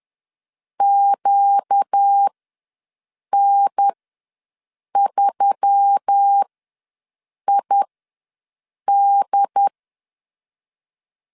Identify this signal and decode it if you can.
Morse code, heard as a keyed tone, about 10 wpm, spaced as for 6 wpm, 800 Hz. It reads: QN3ID
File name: LEZIONI / 5 LBVC / 5 LB ws 10wpm 3616